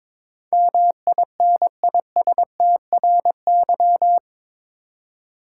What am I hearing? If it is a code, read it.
Morse code, heard as a keyed tone, about 22 wpm, 700 Hz. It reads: MINISTRY
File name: practice / 22wpm / Eight Letter Words - 22wpm